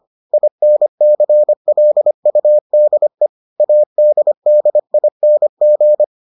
Morse code, heard as a keyed tone, about 25 wpm, 600 Hz. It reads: INCLUDE ADDING